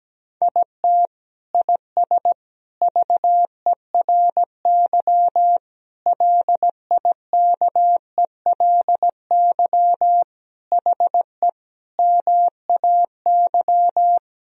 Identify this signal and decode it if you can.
Morse code, heard as a keyed tone, about 17 wpm, 700 Hz. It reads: IT IS VERY LIKELY HE MAY
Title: Sentences from Top 1000 Words - Review Part 4 - 17wpm